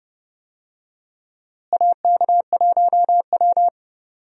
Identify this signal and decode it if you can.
Morse code, heard as a keyed tone, about 31 wpm, 700 Hz. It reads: AK1W
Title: USA Call Signs - 4 Characters - Repeated 3x - 30wpm